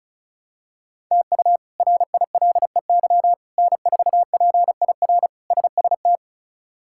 Morse code, heard as a keyed tone, about 36 wpm, 700 Hz. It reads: TU RILEY N4PIR SST
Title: K1USN SST QSO Practice - 35wpm